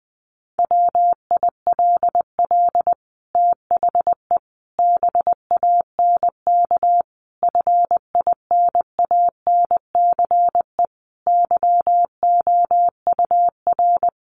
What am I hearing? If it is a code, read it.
Morse code, heard as a keyed tone, about 20 wpm, 700 Hz. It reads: WILL THE BANK FINANCE YOUR
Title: Sentences from Top 1500 Words 20wpm